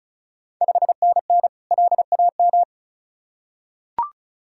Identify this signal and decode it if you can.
Morse code, heard as a keyed tone, about 36 wpm, 690 Hz. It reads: 5NN LAM
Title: Ontario QSO Party Exchange 35wpm